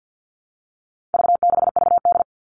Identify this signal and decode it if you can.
Morse code, heard as a keyed tone, about 50 wpm, 700 Hz. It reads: V6VD
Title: All Cut Numbers - 4-Digit 50wpm